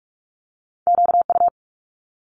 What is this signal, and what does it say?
Morse code, heard as a keyed tone, about 46 wpm, 700 Hz. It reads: QU
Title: Multi-letter Phonemes Quick Fire 45wpm